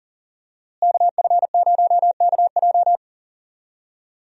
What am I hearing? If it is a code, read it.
Morse code, heard as a keyed tone, about 38 wpm, 700 Hz. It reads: KF0KJ
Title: USA Call Signs - 5 Characters - Repeated 2x - 40wpm